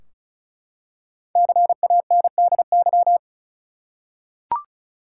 Morse code, heard as a keyed tone, about 34 wpm, 700 Hz. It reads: CANDY
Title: Five Letter Words - 35wpm